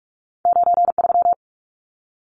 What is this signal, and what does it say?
Morse code, heard as a keyed tone, about 46 wpm, 700 Hz. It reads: 93
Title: Numbers - 2-Digit 45wpm